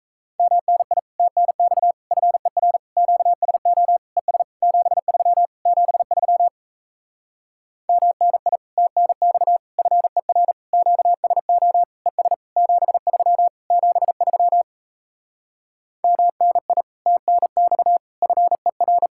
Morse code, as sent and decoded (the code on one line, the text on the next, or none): -- -. .. / - -. -..- / ..-. .-. / --.- ... --- / . ... / --... ...-- / --... ...-- / -- -. .. / - -. -..- / ..-. .-. / --.- ... --- / . ... / --... ...-- / --... ...-- / -- -. .. / - -. -..- / ..-. .-.
MNI TNX FER QSO ES 73 73 MNI TNX FER QSO ES 73 73 MNI TNX FER